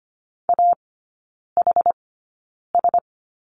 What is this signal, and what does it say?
Morse code, heard as a keyed tone, about 25 wpm, spaced as for 9 wpm, 700 Hz. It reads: AHS
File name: Lesson 17e - Letter u - Three Characters 25wpm